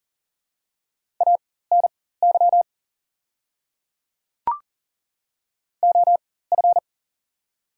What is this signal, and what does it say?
Morse code, heard as a keyed tone, about 40 wpm, spaced as for 19 wpm, 700 Hz. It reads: ANY OF
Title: 4x Character Spacing - Top 200 Words - 40wpm